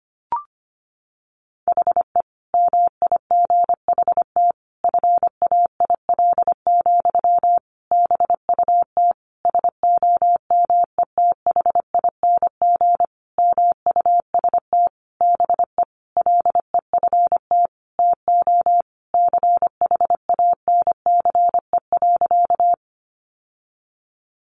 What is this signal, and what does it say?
Morse code, heard as a keyed tone, about 25 wpm, 700 Hz. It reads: HE MIGHT FAIL, BUT SOMETHING MUST BE LEFT TO CHANCE.
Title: Sentences from Top 1100 Words 25wpm